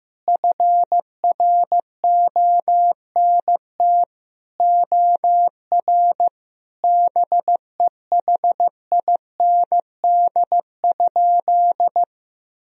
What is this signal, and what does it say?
Morse code, heard as a keyed tone, about 15 wpm, 700 Hz. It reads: FRONT OR BEHIND?
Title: Sentences from Top 1000 Words - Review Part 2 - 15wpm